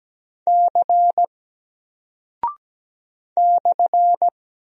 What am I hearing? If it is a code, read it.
Morse code, heard as a keyed tone, about 17 wpm, 700 Hz. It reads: C /
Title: Single Character (Letter:Number:Punctuation:Prosign) 17wpm